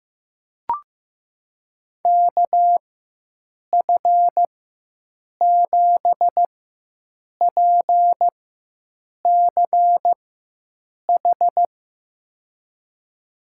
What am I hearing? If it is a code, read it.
Morse code, heard as a keyed tone, about 15 wpm, spaced as for 7 wpm, 700 Hz. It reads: KF7PCH
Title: USA Call Signs - 6 Characters - 4x Character Spacing - 15wpm